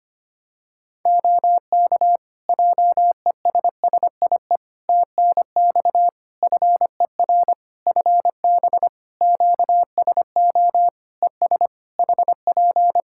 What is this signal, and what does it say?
Morse code, heard as a keyed tone, about 25 wpm, 700 Hz. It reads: OK JESSIE TNX FER FB QSO ES HP
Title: Practice QSO - Exchange 4 - Ending - 25wpm